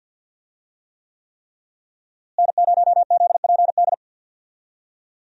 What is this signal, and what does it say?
Morse code, heard as a keyed tone, about 52 wpm, 700 Hz. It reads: N0ZPD
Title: USA Call Signs - 5 Characters - Repeated 3x - 50wpm